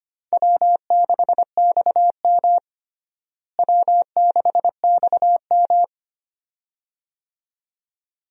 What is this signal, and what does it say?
Morse code, heard as a keyed tone, about 25 wpm, 700 Hz. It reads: W6XM W6XM
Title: USA Call Signs - 4 Characters - Repeated 2x - 25wpm